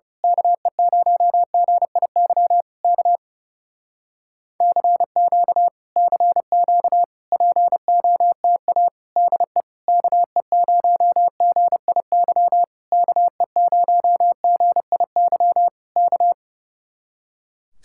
Morse code, heard as a keyed tone, about 28 wpm, 700 Hz. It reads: KE0GIY K CQ CQ POTA DE KE0GIY KE0GIY K
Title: Speed-Racing Common QSO Phrases Encore - 30wpm